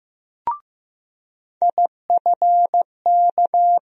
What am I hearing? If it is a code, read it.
Morse code, heard as a keyed tone, about 15 wpm, 700 Hz. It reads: IFK
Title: USA Call Sign Postfixes - 3 Letters 15wpm